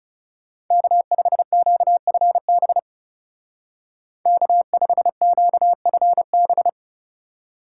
Morse code, heard as a keyed tone, about 35 wpm, 700 Hz. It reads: K5QFB K5QFB
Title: Speed Racing - USA Call Signs 30wpm